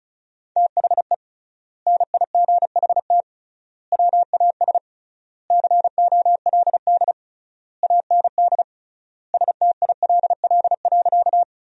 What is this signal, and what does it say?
Morse code, heard as a keyed tone, about 36 wpm, 700 Hz. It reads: THE NIGHT WAS COLD AND STILL.
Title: 3x Word-Spacing - Sentences from Top 400 Words - 35wpm